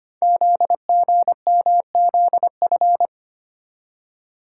Morse code, heard as a keyed tone, about 25 wpm, 700 Hz. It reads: ZGMZF
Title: Code Groups - 5 Letters - 25wpm